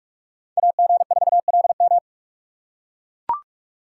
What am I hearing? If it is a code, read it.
Morse code, heard as a keyed tone, about 44 wpm, 700 Hz. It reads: AG4LM